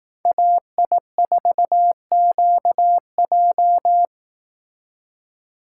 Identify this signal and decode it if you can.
Morse code, heard as a keyed tone, about 18 wpm, 700 Hz. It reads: AI4QJ
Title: Speed Racing - USA Call Signs 15wpm